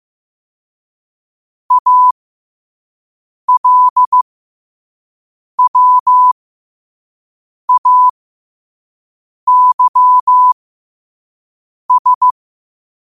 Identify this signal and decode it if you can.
Morse code, heard as a keyed tone, about 15 wpm, spaced as for 5 wpm, 1000 Hz. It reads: ALWAYS
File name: qotdfiles / 5wpm